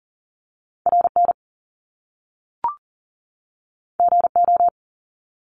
Morse code, heard as a keyed tone, about 40 wpm, 700 Hz. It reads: RN GO
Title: Top Two Letter Combinations in Words 40wpm